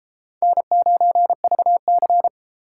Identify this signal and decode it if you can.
Morse code, heard as a keyed tone, about 32 wpm, 700 Hz. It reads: N9VC